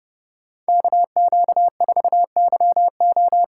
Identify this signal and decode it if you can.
Morse code, heard as a keyed tone, about 32 wpm, 700 Hz. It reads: KQ4YO